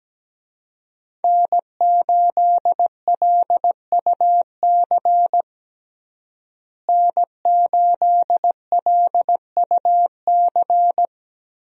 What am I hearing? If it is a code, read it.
Morse code, heard as a keyed tone, about 17 wpm, 700 Hz. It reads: N8LUC N8LUC
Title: USA Call Signs - 5 Characters - Repeated 2x - 17wpm